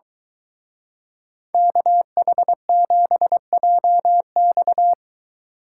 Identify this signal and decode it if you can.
Morse code, heard as a keyed tone, about 22 wpm, 700 Hz. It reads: KH7JX